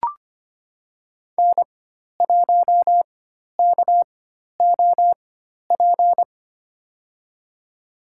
Morse code, heard as a keyed tone, about 25 wpm, spaced as for 12 wpm, 700 Hz. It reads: N1KOP
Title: USA Call Signs - 5 Characters - 4x Character Spacing - 25wpm